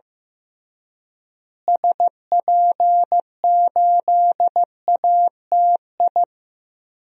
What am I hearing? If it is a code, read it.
Morse code, heard as a keyed tone, about 15 wpm, 700 Hz. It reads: SP8ATI
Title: DXCC-100 Call Signs - 6 Characters - Repeated 3x - 15wpm